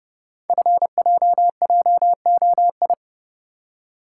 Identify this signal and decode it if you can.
Morse code, heard as a keyed tone, about 30 wpm, 700 Hz. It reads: FJJOI
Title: Code Groups - 5 Letters - 30wpm